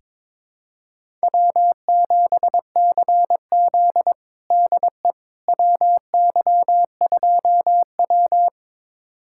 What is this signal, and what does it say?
Morse code, heard as a keyed tone, about 22 wpm, 700 Hz. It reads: W7CZ DE WY2W
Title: USA Callsign DE Callsign - 4 Characters - 22wpm